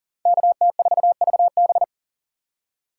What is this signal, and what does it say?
Morse code, heard as a keyed tone, about 37 wpm, 700 Hz. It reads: KT4VB
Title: USA Call Signs - 5 Characters - 3x Character Spacing - 40wpm